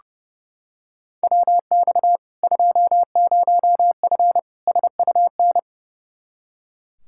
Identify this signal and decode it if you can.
Morse code, heard as a keyed tone, about 30 wpm, 700 Hz. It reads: WX 20F SUN
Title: Practice QSO - Exchange 2 - Info - 30wpm